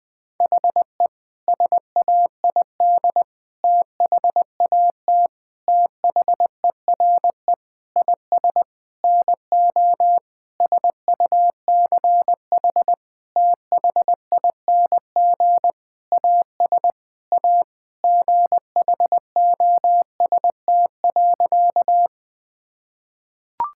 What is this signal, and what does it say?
Morse code, heard as a keyed tone, about 20 wpm, 700 Hz. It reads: HE SAID THAT THERE IS NO SUCH THING AS A GHOST.
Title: Sentences with 8-Word Phrases - 20wpm